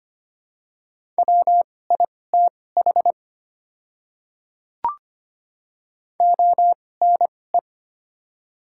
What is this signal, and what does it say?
Morse code, heard as a keyed tone, about 25 wpm, spaced as for 18 wpm, 700 Hz. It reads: WITH ONE
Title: 2x Character Spacing - Top 300 Words - 25wpm